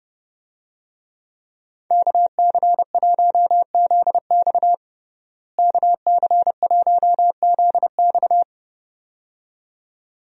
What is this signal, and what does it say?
Morse code, heard as a keyed tone, about 29 wpm, 700 Hz. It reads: KC1ZX KC1ZX
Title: USA Call Signs - 5 Characters - Repeated 2x - 30wpm